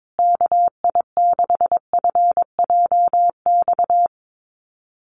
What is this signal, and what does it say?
Morse code, heard as a keyed tone, about 22 wpm, 700 Hz. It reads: KI6FJX
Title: USA Call Signs Repeated 3x 22wpm